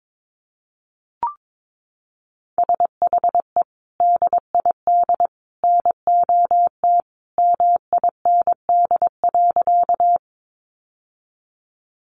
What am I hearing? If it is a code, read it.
Morse code, heard as a keyed tone, about 22 wpm, 700 Hz. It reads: SHE DID NOT MIND.
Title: Sentences from Top 600 Words 22wpm